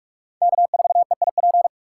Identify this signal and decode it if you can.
Morse code, heard as a keyed tone, about 47 wpm, 700 Hz. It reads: K4EIP
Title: USA Call Signs - 8x Character Spacing - 45wpm